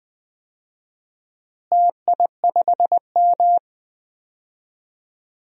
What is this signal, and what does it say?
Morse code, heard as a keyed tone, about 20 wpm, 700 Hz. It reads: TI5M
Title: DXCC-100 Call Signs - 4 Characters - Repeated 2x - 20wpm